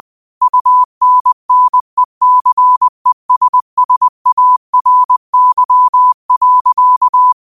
Morse code, heard as a keyed tone, about 20 wpm, 1000 Hz. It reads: UNNECESSARY.
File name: qotdfiles / 20wpm